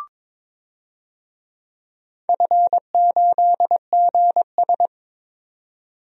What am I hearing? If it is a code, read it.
Morse code, heard as a keyed tone, about 22 wpm, 700 Hz. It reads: F8GS
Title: DXCC-100 Call Signs - 4 Characters - 22wpm